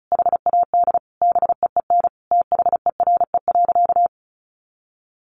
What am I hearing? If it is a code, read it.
Morse code, heard as a keyed tone, about 34 wpm, 700 Hz. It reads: HAD BEEN THERE.
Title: Sentences from Top 300 Words 35wpm